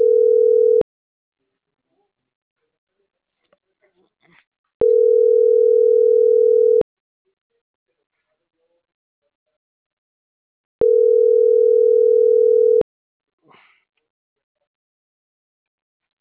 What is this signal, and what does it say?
Ringback tone (North America), 440 + 480 Hz.